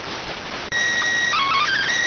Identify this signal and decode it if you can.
SSTV (slow-scan television), PD 290.